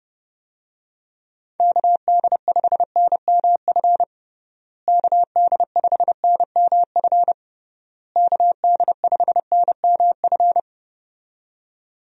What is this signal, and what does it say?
Morse code, heard as a keyed tone, about 30 wpm, 700 Hz. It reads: KD5NMF KD5NMF KD5NMF
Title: USA Call Signs - 6 Characters - Repeated 3x - 30wpm